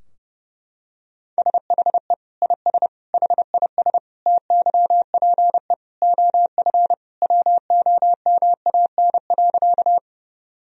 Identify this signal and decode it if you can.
Morse code, heard as a keyed tone, about 32 wpm, 700 Hz. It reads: SHE IS HIS TYPE OF WOMAN.
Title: Sentences from Top 1000 Words - Review Part 3 - 30wpm